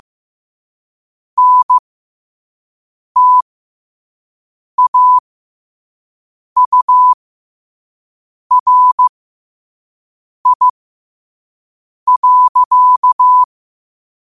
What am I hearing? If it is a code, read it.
Morse code, heard as a keyed tone, about 15 wpm, spaced as for 5 wpm, 1000 Hz. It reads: NTAURI.
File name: qotdfiles / 5wpm